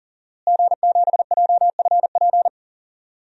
Morse code, heard as a keyed tone, about 40 wpm, 700 Hz. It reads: G7JFP
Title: DXCC-100 Call Signs - 5 Characters - Repeated 2x - 40wpm